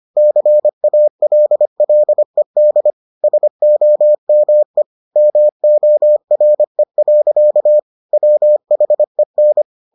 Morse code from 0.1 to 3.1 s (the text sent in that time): CALLED